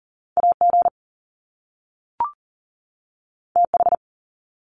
Morse code, heard as a keyed tone, about 40 wpm, 700 Hz. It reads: AG TH
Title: Top Two Letter Combinations in Words 40wpm